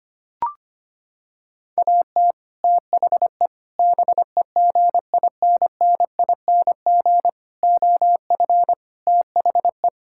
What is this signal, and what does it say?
Morse code, heard as a keyed tone, about 25 wpm, 700 Hz. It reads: AT THE BEGINNING OF THE
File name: practice / 25wpm / Sentences with 3-Word Phrases - 25wpm